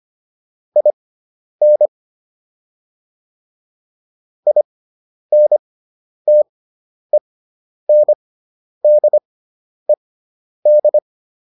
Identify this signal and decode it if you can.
Morse code, heard as a keyed tone, about 25 wpm, spaced as for 10 wpm, 600 Hz. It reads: IN INTENDED